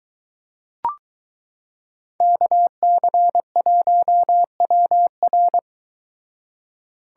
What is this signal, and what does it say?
Morse code, heard as a keyed tone, about 23 wpm, 700 Hz. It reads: KC1WR